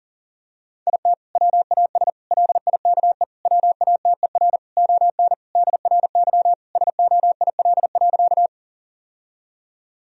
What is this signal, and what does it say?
Morse code, heard as a keyed tone, about 40 wpm, 700 Hz. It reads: IT WAS LIKE WATER ON DRY SOIL.
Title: Sentences from Top 1000 Words - Review Part 4 - 40wpm